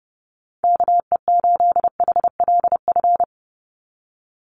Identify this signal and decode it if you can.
Morse code, heard as a keyed tone, about 30 wpm, 700 Hz. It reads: KE8HLF
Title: USA Call Signs 30wpm